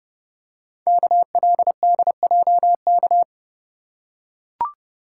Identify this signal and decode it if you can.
Morse code, heard as a keyed tone, about 30 wpm, 690 Hz. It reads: KLDJK
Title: Code Groups - 5 Letters - 30wpm